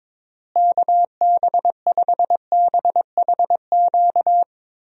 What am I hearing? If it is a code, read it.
Morse code, heard as a keyed tone, about 22 wpm, 700 Hz. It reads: KB5BHQ
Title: USA Call Signs - 6 Characters - Repeated 3x - 22wpm